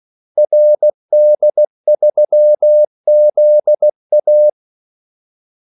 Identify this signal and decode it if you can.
Morse code, heard as a keyed tone, about 16 wpm, 600 Hz. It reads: RD3ZA
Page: Kaldesignaler 16 wpm